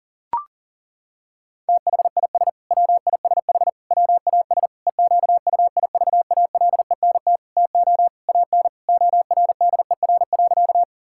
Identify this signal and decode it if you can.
Morse code, heard as a keyed tone, about 40 wpm, 700 Hz. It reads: THIS WISH WAS EQUIVALENT TO AN ORDER.
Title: Sentences from Top 1600 Words 40wpm